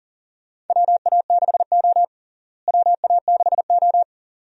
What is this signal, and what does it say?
Morse code, heard as a keyed tone, about 38 wpm, 700 Hz. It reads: WA6O WA6O
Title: USA Call Signs - 4 Characters - Repeated 2x - 40wpm